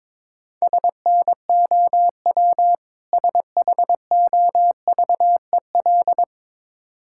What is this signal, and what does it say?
Morse code, heard as a keyed tone, about 22 wpm, 700 Hz. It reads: SNOW SHOVEL
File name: Easy Words - Speed-Racing with Hints - 22wpm